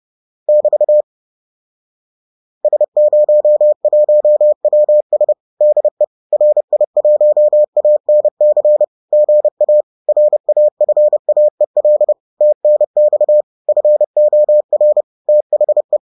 Morse code, heard as a keyed tone, about 30 wpm, 600 Hz. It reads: <BT> S01WS DE RI1ANC GA RAFAEL TNX FOR THE